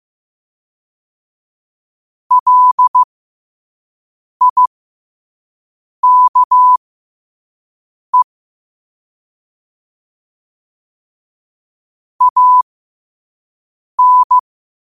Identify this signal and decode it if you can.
Morse code, heard as a keyed tone, about 15 wpm, spaced as for 5 wpm, 1000 Hz. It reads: LIKE AN
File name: qotdfiles / 5wpm